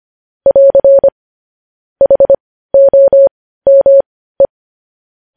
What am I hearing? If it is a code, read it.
Morse code, heard as a keyed tone, about 25 wpm, spaced as for 15 wpm, 550 Hz. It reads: <AR> HOME
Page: news headlines in morse code at 15 wpm